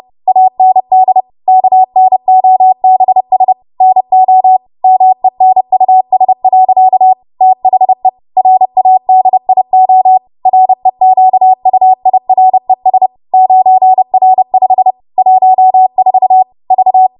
Morse code, heard as a keyed tone, about 30 wpm, 750 Hz. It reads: AND KNOBS NO MENUS. THE RADIO REQUIRES 9R5 14 V